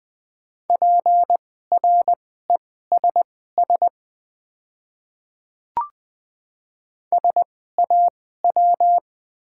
Morse code, heard as a keyed tone, about 20 wpm, spaced as for 15 wpm, 690 Hz. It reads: PRESS SAW
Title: 2x Character Spacing - Top 500 Words - 20wpm